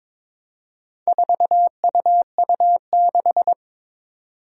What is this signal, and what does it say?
Morse code, heard as a keyed tone, about 22 wpm, 700 Hz. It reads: 4UU6
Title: Four Challenging Letter-Number Combinations 22wpm